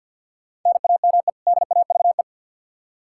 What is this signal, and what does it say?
Morse code, heard as a keyed tone, about 48 wpm, 700 Hz. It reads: NAME DAVE
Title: Practice QSO - Exchange 1 - Beginning - 50wpm